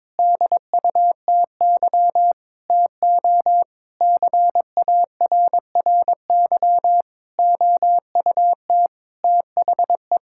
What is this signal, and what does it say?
Morse code, heard as a keyed tone, about 22 wpm, 700 Hz. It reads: DUTY TO CARRY OUT THE
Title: Sentences from Top 1600 Words 22wpm